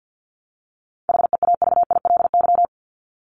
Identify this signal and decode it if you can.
Morse code, heard as a keyed tone, about 50 wpm, 700 Hz. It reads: HEAVILY
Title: LY Suffix - Top 100 Words with Base Words 50wpm